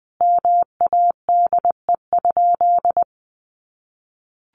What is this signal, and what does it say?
Morse code, heard as a keyed tone, about 20 wpm, 700 Hz. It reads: MADE?
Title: Sentences from Top 1200 Words 20wpm